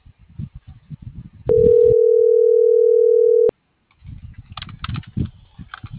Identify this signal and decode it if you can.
Ringback tone (North America), 440 + 480 Hz.